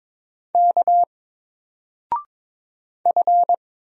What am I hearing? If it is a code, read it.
Morse code, heard as a keyed tone, about 22 wpm, 700 Hz. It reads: K F
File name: Single-Letter-Number-22wpm